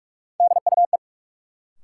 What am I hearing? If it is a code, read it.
Morse code, heard as a keyed tone, about 44 wpm, 700 Hz. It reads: DUE
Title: Sets of 5 Words Encore - New Word Drill - 45wpm